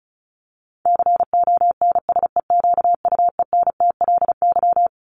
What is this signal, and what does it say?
Morse code, heard as a keyed tone, about 34 wpm, 700 Hz. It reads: CONSEQUENTLY
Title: LY Suffix - Top 100 Words with Base Words 35wpm